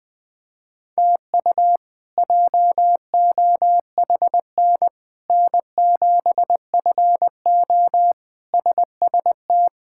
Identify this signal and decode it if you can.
Morse code, heard as a keyed tone, about 20 wpm, 700 Hz. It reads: TU JOHN N7FO SST
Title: K1USN SST QSO Practice - 20wpm